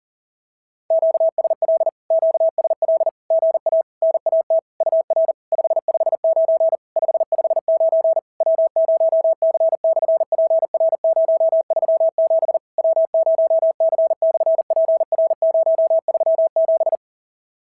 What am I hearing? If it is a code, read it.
Morse code, heard as a keyed tone, about 40 wpm, 650 Hz. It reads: QSL QSL GA NAT UR 559 559 W0C/PR037 W0C/PR037
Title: SOTA QSO Practice 40wpm